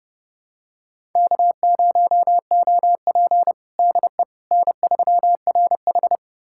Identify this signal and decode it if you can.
Morse code, heard as a keyed tone, about 30 wpm, 700 Hz. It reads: K0OP DE N3RH
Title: USA Callsign DE Callsign - 4 Characters - 30wpm